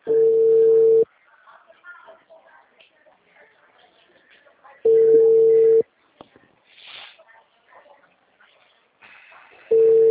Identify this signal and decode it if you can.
Ringback tone (China, 450 Hz), with speech.
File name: out- -2002- - - .5